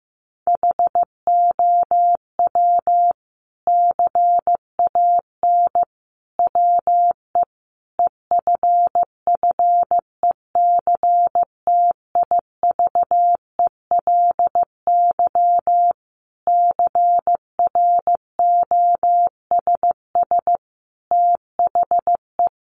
Morse code, heard as a keyed tone, about 15 wpm, 700 Hz. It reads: HOW CAN WE EFFECTIVELY CROSS THE
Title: Sentences from Top 1400 Words 15wpm